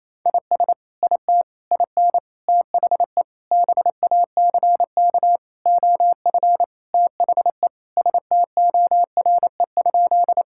Morse code, heard as a keyed tone, about 27 wpm, 700 Hz. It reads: IS IT IN THE BACK OF THE STORE?
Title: Sentences from Top 1000 Words - Review Part 5 - 28wpm